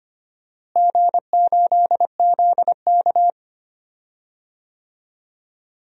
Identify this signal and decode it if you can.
Morse code, heard as a keyed tone, about 25 wpm, 700 Hz. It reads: G8ZK
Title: DXCC-100 Call Signs - 4 Characters - Repeated 2x - 25wpm